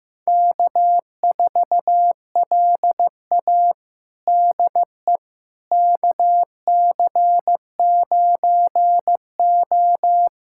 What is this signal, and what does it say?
Morse code, heard as a keyed tone, about 15 wpm, 700 Hz. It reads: K4LA DE KC9O
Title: USA Callsign DE Callsign - 4 Characters - 15wpm